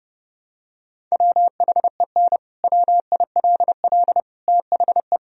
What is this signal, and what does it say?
Morse code, heard as a keyed tone, about 30 wpm, 700 Hz. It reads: WHEN WILL THE
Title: Sentences from Top 1000 Words - Review Part 3 - 30wpm